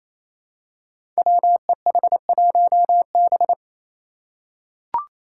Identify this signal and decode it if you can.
Morse code, heard as a keyed tone, about 28 wpm, 700 Hz. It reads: WEH1B
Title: Five Challenging Letter-Number Combinations 28wpm